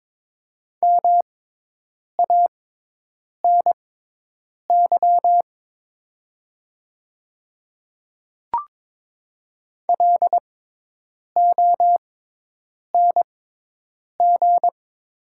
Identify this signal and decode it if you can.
Morse code, heard as a keyed tone, about 22 wpm, spaced as for 8 wpm, 700 Hz. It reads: MANY LONG
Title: 6x Character Spacing - Top 100 Words - 22wpm